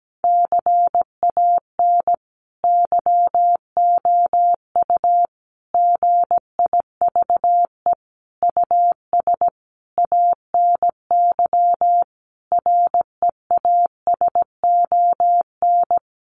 Morse code, heard as a keyed tone, about 17 wpm, 700 Hz. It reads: CAN YOU GIVE US ANY REASON
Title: Sentences from Top 1100 Words 17wpm